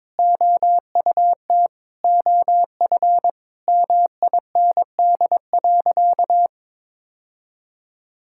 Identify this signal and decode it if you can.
Morse code, heard as a keyed tone, about 22 wpm, 700 Hz. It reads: OUT OF MIND.
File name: Sentences with 7-Word Phrases - 22wpm